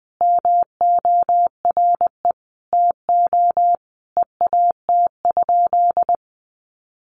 Morse code, heard as a keyed tone, about 20 wpm, 700 Hz. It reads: MORE TO EAT?
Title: Sentences from Top 300 Words 20wpm